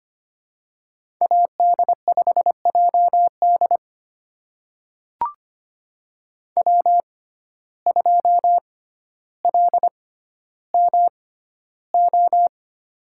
Morse code, heard as a keyed tone, about 25 wpm, spaced as for 13 wpm, 700 Hz. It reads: AD5JD W2LMO
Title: USA Call Signs - 5 Characters - 6x Character Spacing - 25wpm